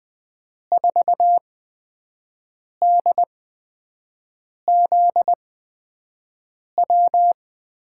Morse code, heard as a keyed tone, about 20 wpm, spaced as for 5 wpm, 700 Hz. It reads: 4DZW
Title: USA Call Signs - 5 Characters - 8x Character Spacing - 20wpm